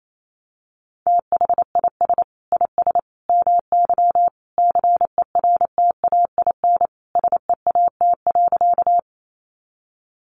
Morse code, heard as a keyed tone, about 28 wpm, 700 Hz. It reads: THIS IS MY CERTAIN SEAT.
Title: Sentences from Top 1500 Words 28wpm